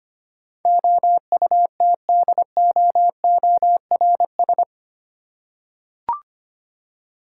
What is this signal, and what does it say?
Morse code, heard as a keyed tone, about 25 wpm, 700 Hz. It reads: OUTDOORS
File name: 8 Letter Words - First 3 Letters Given - 25wpm